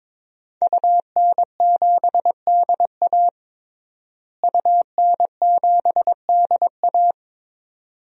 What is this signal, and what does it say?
Morse code, heard as a keyed tone, about 22 wpm, 700 Hz. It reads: UN7DA UN7DA
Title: DXCC-100 Call Signs - 5 Characters - Repeated 2x - 22wpm